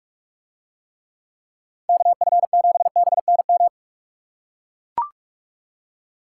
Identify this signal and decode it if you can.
Morse code, heard as a keyed tone, about 40 wpm, 700 Hz. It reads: KF7BNM